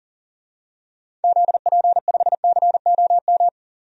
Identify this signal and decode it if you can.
Morse code, heard as a keyed tone, about 40 wpm, 700 Hz. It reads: ZP5COM
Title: DXCC-100 Call Signs - 6 Characters - 40wpm